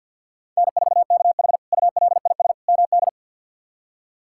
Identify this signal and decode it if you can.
Morse code, heard as a keyed tone, about 53 wpm, 700 Hz. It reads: N4KH ULIS MD
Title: K1USN SST QSO Practice - 50wpm